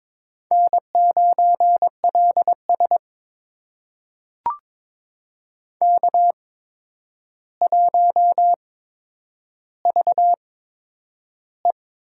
Morse code, heard as a keyed tone, about 22 wpm, spaced as for 9 wpm, 700 Hz. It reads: N9LS K1VE
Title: USA Call Signs - 4 Characters - 8x Character Spacing - 22wpm